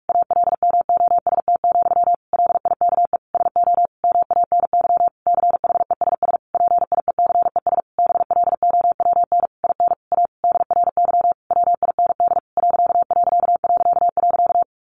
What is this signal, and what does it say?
Morse code, heard as a keyed tone, about 44 wpm, 700 Hz. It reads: ALMOST, LIKE SO MANY CHESS PIECES BLOWN IN A DRY WIND ....
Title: A Sound Of Thunder 45wpm